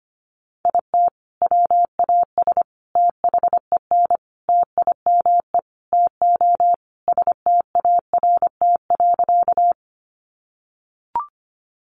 Morse code, heard as a keyed tone, about 25 wpm, 700 Hz. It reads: IT WAS THEN TIME TO START.
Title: Sentences from Top 300 Words 25wpm